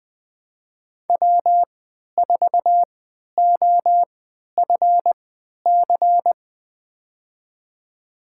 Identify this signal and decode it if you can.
Morse code, heard as a keyed tone, about 21 wpm, spaced as for 11 wpm, 700 Hz. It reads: W4OFC
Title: USA Call Signs - 5 Characters - 3x Character Spacing - 20wpm